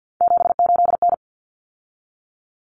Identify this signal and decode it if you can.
Morse code, heard as a keyed tone, about 50 wpm, 700 Hz. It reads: 78N
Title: Common Cut Numbers - 3-Digit 50wpm